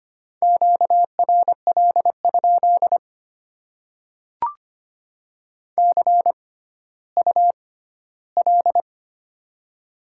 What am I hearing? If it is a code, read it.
Morse code, heard as a keyed tone, about 25 wpm, spaced as for 14 wpm, 700 Hz. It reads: QRL? CUL
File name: Top 90 QSO Elements - 6x Character Spacing - 25wpm